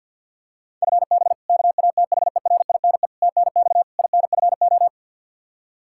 Morse code, heard as a keyed tone, about 48 wpm, 700 Hz. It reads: FB KATHERINE TNX INFO
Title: Practice QSO - Exchange 3 - More Info - 50wpm